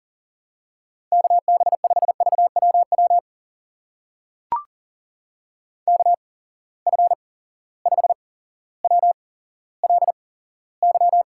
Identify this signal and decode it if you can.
Morse code, heard as a keyed tone, about 40 wpm, spaced as for 17 wpm, 700 Hz. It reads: KB5VWW KF5WLY
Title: USA Call Signs - 6 Characters - 8x Character Spacing - 40wpm